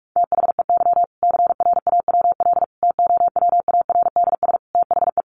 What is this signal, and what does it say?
Morse code, heard as a keyed tone, about 44 wpm, 700 Hz. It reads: THEY CRAWL TOWARDS THE
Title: Easy Phrases 45wpm